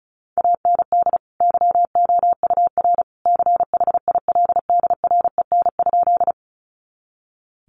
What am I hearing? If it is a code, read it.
Morse code, heard as a keyed tone, about 34 wpm, 700 Hz. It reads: AND YOUR CHILDREN?
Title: Sentences from Top 300 Words 35wpm